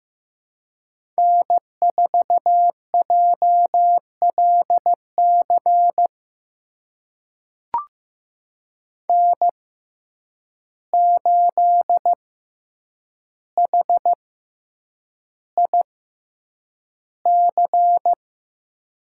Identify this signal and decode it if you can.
Morse code, heard as a keyed tone, about 15 wpm, spaced as for 8 wpm, 700 Hz. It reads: N4JLC N8HIC
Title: USA Call Signs - 6x Character Spacing - 15wpm